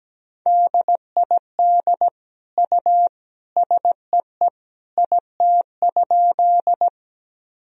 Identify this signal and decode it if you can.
Morse code, heard as a keyed tone, about 17 wpm, 700 Hz. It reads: DID U SEE IT?
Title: Sentences from Top 100 Words and Abbreviations 17wpm